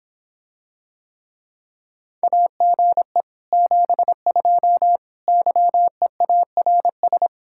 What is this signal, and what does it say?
Morse code, heard as a keyed tone, about 26 wpm, 700 Hz. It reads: AGE 72 YEARS